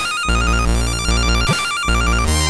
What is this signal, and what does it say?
There is no code.